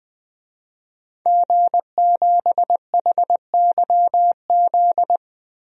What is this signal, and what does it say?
Morse code, heard as a keyed tone, about 21 wpm, 700 Hz. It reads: G7HYZ
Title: DXCC-100 Call Signs - 5 Characters - 20wpm